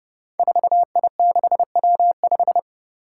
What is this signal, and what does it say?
Morse code, heard as a keyed tone, about 30 wpm, 700 Hz. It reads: 4I6W5
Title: Five Challenging Letter-Number Combinations 30wpm